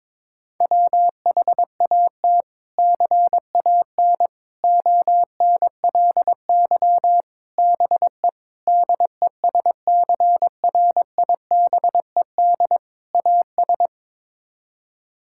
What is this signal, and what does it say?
Morse code, heard as a keyed tone, about 22 wpm, 700 Hz. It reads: WHAT CAN ONLY BE DESCRIBED AS
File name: practice / 22wpm / Sets of 6 Words - 22wpm